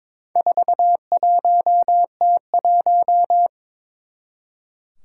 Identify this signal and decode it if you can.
Morse code, heard as a keyed tone, about 22 wpm, 700 Hz. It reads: 41T1
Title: Four Challenging Letter-Number Combinations 22wpm